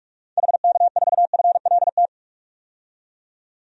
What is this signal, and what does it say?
Morse code, heard as a keyed tone, about 46 wpm, 700 Hz. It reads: HK4FLT
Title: DXCC-100 Call Signs - 6 Characters - 45wpm